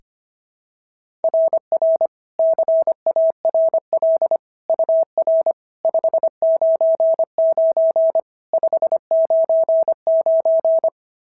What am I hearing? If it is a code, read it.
Morse code, heard as a keyed tone, about 23 wpm, 650 Hz. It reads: RR CARL UR 599 599
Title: SOTA QSO Practice 25wpm